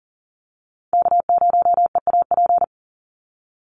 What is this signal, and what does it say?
Morse code, heard as a keyed tone, about 40 wpm, 700 Hz. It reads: K0EAP
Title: NA SOTA Chaser Callsigns 40wpm